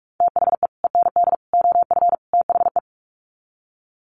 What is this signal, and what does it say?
Morse code, heard as a keyed tone, about 44 wpm, 700 Hz. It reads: THE END OF THE
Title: Sets of 4 Words 45wpm